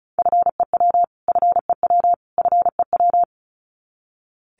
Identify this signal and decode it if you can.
Morse code, heard as a keyed tone, about 34 wpm, 700 Hz. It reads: FEW FEW FEW
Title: Top 500 Words Repeated 35wpm